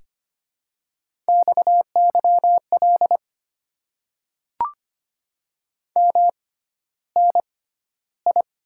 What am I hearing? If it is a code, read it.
Morse code, heard as a keyed tone, about 25 wpm, spaced as for 13 wpm, 700 Hz. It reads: XYL MNI
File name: Top 90 QSO Elements - 6x Character Spacing - 25wpm